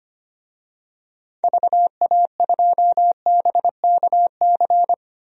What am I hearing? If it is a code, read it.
Morse code, heard as a keyed tone, about 25 wpm, 700 Hz. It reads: VA2BKC
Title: DXCC-100 Call Signs - 6 Characters - 25wpm